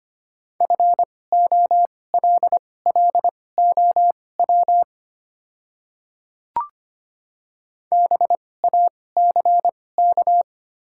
Morse code, heard as a keyed tone, about 25 wpm, spaced as for 18 wpm, 700 Hz. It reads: FOLLOW BACK
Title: 2x Character Spacing - Top 300 Words - 25wpm